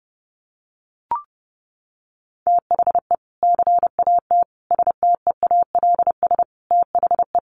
Morse code, heard as a keyed tone, about 30 wpm, 700 Hz. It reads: THE CAT STEALS THE
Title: Easy Phrases 30wpm